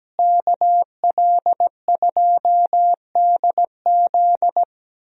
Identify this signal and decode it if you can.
Morse code, heard as a keyed tone, about 17 wpm, 700 Hz. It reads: KL2DZ
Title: USA Call Signs - 5 Characters - Repeated 2x - 17wpm